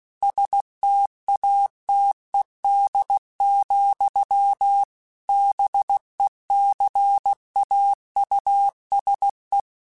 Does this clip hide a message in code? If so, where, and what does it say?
Yes: Morse code from 0.2 to 9.6 s, heard as a keyed tone, about 16 wpm, 800 Hz. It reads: STATED, BECAUSE